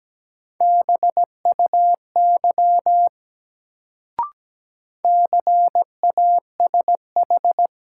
Morse code, heard as a keyed tone, about 17 wpm, 700 Hz. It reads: BUY CASH
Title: Sentences from Top 700 Words - New Word Drill - 17wpm